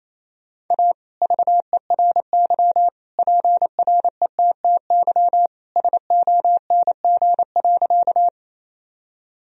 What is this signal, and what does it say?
Morse code, heard as a keyed tone, about 28 wpm, 700 Hz. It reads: A VERY PRETTY SONG.
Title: Sentences from Top 1000 Words - Review Part 2 - 28wpm